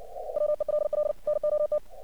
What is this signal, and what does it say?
Morse code, heard as a keyed tone, about 47 wpm, 600 Hz. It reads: WELL NOT
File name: archived / CRSnet